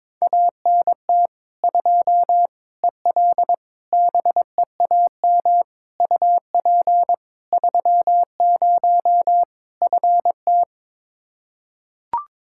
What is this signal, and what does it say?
Morse code, heard as a keyed tone, about 21 wpm, 700 Hz. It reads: ANT 2 EL BEAM UP 30 FT
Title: Practice QSO - Exchange 2 - Info - 22wpm